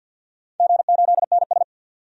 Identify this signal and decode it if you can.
Morse code, heard as a keyed tone, about 51 wpm, 700 Hz. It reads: G8NS
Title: DXCC-100 Call Signs - 4 Characters - 50wpm